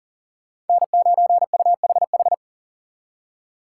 Morse code, heard as a keyed tone, about 52 wpm, 700 Hz. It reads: N9UHH